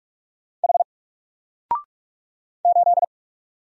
Morse code, heard as a keyed tone, about 44 wpm, 700 Hz. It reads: H 8
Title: Single Letter-Number 45wpm